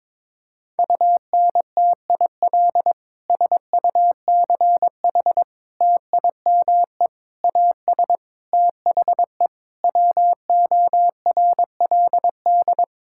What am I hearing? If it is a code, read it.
Morse code, heard as a keyed tone, about 22 wpm, 700 Hz. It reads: UNTIL SUCH TIME AS THE WORLD
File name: Sentences with 9-Word Phrases - 22wpm